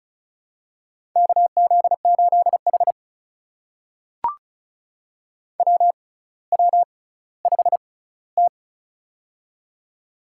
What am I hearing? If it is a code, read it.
Morse code, heard as a keyed tone, about 36 wpm, spaced as for 18 wpm, 700 Hz. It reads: KZ8H WW5T
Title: USA Call Signs - 4 Characters - 6x Character Spacing - 35wpm